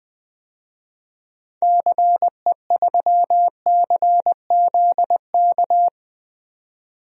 Morse code, heard as a keyed tone, about 20 wpm, 700 Hz. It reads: CE3CZK
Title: DXCC-100 Call Signs - 6 Characters - 20wpm